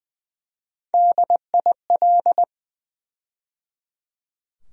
Morse code, heard as a keyed tone, about 20 wpm, 700 Hz. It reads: DIL
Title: USA Call Sign Postfixes - 3 Letters 20wpm